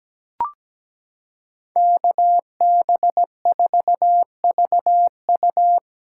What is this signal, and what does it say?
Morse code, heard as a keyed tone, about 17 wpm, 700 Hz. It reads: KB4VU
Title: K1USN SST QSO Practice - 17wpm